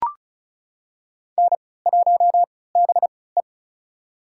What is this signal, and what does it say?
Morse code, heard as a keyed tone, about 36 wpm, spaced as for 20 wpm, 690 Hz. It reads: N1BE
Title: USA Call Signs - 3x Character Spacing - 35wpm